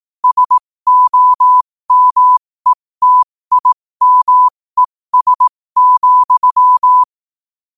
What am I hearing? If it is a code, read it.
Morse code, heard as a keyed tone, about 18 wpm, spaced as for 15 wpm, 1000 Hz. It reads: SOMETIMES,
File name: qotdfiles / 15wpm